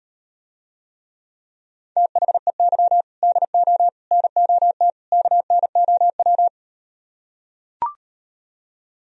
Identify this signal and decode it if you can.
Morse code, heard as a keyed tone, about 38 wpm, 700 Hz. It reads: THEY DO NOT KNOW